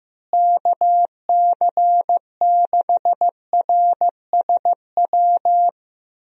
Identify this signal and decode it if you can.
Morse code, heard as a keyed tone, about 15 wpm, 700 Hz. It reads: KC6RSW
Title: USA Call Signs - 6 Characters - Repeated 3x - 15wpm